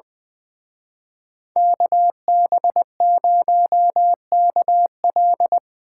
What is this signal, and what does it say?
Morse code, heard as a keyed tone, about 22 wpm, 700 Hz. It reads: KB0KL
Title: Speed Racing - USA Call Signs 20wpm